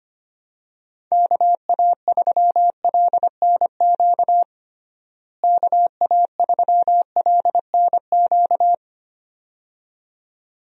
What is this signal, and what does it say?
Morse code, heard as a keyed tone, about 25 wpm, 700 Hz. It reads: KA3LNQ KA3LNQ
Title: USA Call Signs - 6 Characters - Repeated 2x - 25wpm